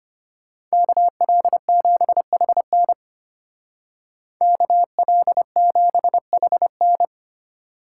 Morse code, heard as a keyed tone, about 33 wpm, 700 Hz. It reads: KL7HN KL7HN